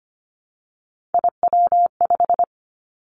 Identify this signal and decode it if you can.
Morse code, heard as a keyed tone, about 25 wpm, 700 Hz. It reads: IW5
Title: Three Challenging Letter-Number Combinations 25wpm